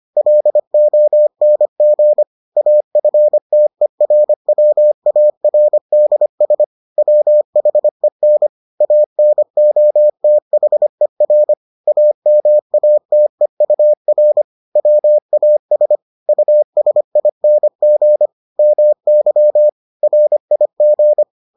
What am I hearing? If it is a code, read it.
Morse code, heard as a keyed tone, about 25 wpm, 600 Hz. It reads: LONG AFTERWARDS WHEN ANOTHER AMATEUR WAS USING MY RIG